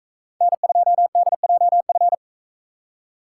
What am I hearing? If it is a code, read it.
Morse code, heard as a keyed tone, about 42 wpm, 700 Hz. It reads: N2DJF